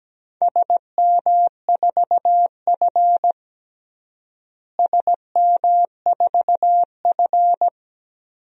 Morse code, heard as a keyed tone, about 17 wpm, 700 Hz. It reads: SM4F SM4F
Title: DXCC-100 Call Signs - 4 Characters - Repeated 3x - 17wpm